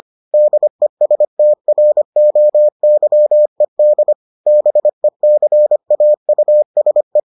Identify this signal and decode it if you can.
Morse code, heard as a keyed tone, about 25 wpm, 600 Hz. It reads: DESTROYED BECAUSE